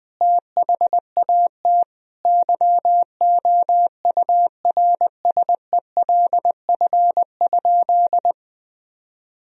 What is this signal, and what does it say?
Morse code, heard as a keyed tone, about 20 wpm, 700 Hz. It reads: THAT YOURSELF?
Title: Sentences from Top 1000 Words - Review Part 1 - 20wpm